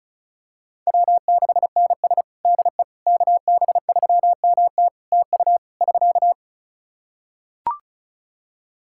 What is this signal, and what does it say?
Morse code, heard as a keyed tone, about 36 wpm, 700 Hz. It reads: W6NS DE KB3MT TU <SK>
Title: Practice QSO - Exchange 4 - Ending - 35wpm